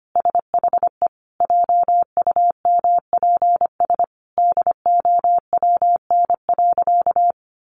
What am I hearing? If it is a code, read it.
Morse code, heard as a keyed tone, about 25 wpm, 700 Hz. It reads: SHE JUMPS DOWN.
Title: Easy Phrases 25wpm